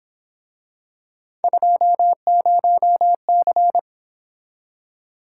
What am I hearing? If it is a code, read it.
Morse code, heard as a keyed tone, about 26 wpm, 700 Hz. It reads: 20C
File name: Speed-Racing - Three Letter-Number Combinations - 22wpm